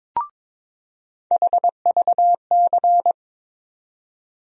Morse code, heard as a keyed tone, about 22 wpm, 700 Hz. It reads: HVC
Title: USA Call Sign Postfixes - 3 Letters 22wpm